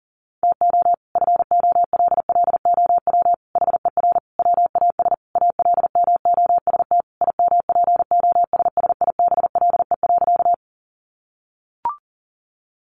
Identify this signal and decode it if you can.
Morse code, heard as a keyed tone, about 40 wpm, 700 Hz. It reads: TO FOLLOW HER WAS ALMOST IMPOSSIBLE.
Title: Sentences from Top 1100 Words 40wpm